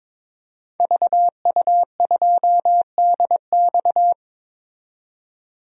Morse code, heard as a keyed tone, about 22 wpm, 700 Hz. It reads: VU2DX
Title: DXCC-100 Call Signs - 5 Characters - 22wpm